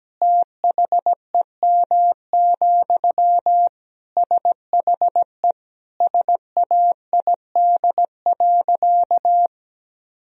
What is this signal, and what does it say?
Morse code, heard as a keyed tone, about 17 wpm, 700 Hz. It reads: THEM, SHE SAID.
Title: Sentences from Top 1000 Words - Review Part 2 - 17wpm